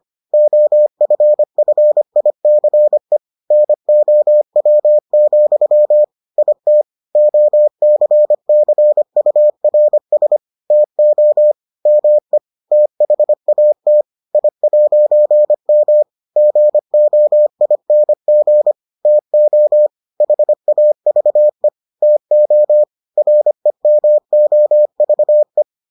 Morse code, heard as a keyed tone, about 25 wpm, 600 Hz. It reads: OFFICE NOW, IT OCCURS TO ME THAT I'M GOING TO HAVE TO REMOVE